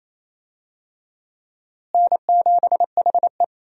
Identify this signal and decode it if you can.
Morse code, heard as a keyed tone, about 28 wpm, 700 Hz. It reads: N7HE
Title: USA Call Signs - 4 Characters - Repeated 3x - 28wpm